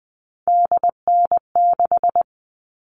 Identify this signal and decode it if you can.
Morse code, heard as a keyed tone, about 20 wpm, 700 Hz. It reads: DN6
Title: All Cut Numbers - 3-Digit 20wpm